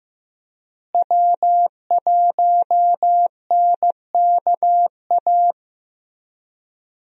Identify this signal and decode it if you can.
Morse code, heard as a keyed tone, about 15 wpm, 700 Hz. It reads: W1NKA
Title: USA Call Signs - 5 Characters - Repeated 3x - 15wpm